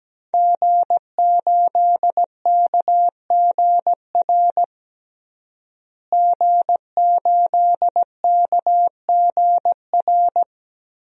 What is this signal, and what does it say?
Morse code, heard as a keyed tone, about 17 wpm, 700 Hz. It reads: G8KGR G8KGR
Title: DXCC-100 Call Signs - 5 Characters - Repeated 2x - 17wpm